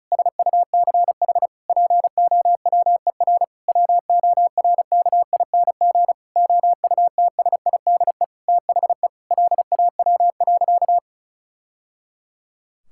Morse code, heard as a keyed tone, about 36 wpm, 700 Hz. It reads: SUCH POWER WORKING OUTSIDE THE LAW.
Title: Sentences from Top 1000 Words - Review Part 6 - 35wpm